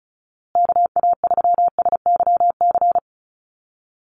Morse code, heard as a keyed tone, about 34 wpm, 700 Hz. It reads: KA3SYC
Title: USA Call Signs Repeated 2x 35wpm